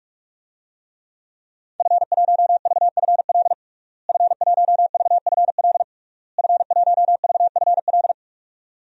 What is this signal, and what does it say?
Morse code, heard as a keyed tone, about 46 wpm, 700 Hz. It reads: F1VFL F1VFL F1VFL
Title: DXCC-100 Call Signs - 5 Characters - Repeated 3x - 45wpm